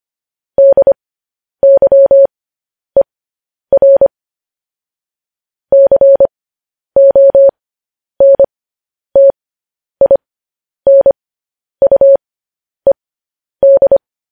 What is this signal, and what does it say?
Morse code, heard as a keyed tone, about 25 wpm, spaced as for 10 wpm, 550 Hz. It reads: DYER CONTINUED
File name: podcast / News headlines-1x-25x10